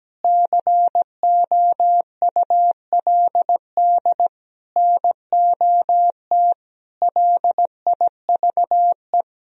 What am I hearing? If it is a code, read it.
Morse code, heard as a keyed tone, about 17 wpm, 700 Hz. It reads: COULD NOT LIVE
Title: Sentences from Top 1000 Words - Review Part 3 - 17wpm